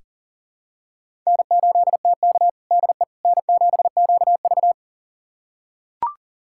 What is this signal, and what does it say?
Morse code, heard as a keyed tone, about 40 wpm, 700 Hz. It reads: N8TK DE N7QV
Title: USA Callsign DE Callsign - 4 Characters - 40wpm